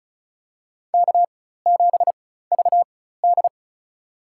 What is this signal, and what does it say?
Morse code, heard as a keyed tone, about 35 wpm, spaced as for 16 wpm, 700 Hz. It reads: K7VD
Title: USA Call Signs - 4 Characters - 4x Character Spacing - 35wpm